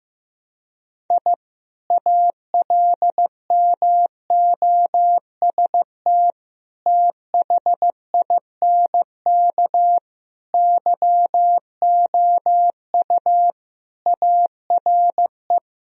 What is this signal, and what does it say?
Morse code, heard as a keyed tone, about 15 wpm, 700 Hz. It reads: I ALMOST THINK YOU ARE
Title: Sentences from Top 1000 Words - Review Part 4 - 15wpm